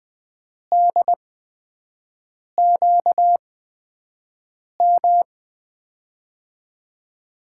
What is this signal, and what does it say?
Morse code, heard as a keyed tone, about 20 wpm, spaced as for 5 wpm, 700 Hz. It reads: DQM
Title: USA Call Signs - 6 Characters - 8x Character Spacing - 20wpm